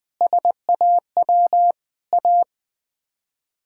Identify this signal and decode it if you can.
Morse code, heard as a keyed tone, about 19 wpm, 700 Hz. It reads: SAW A
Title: Sets of 8 Words - 20wpm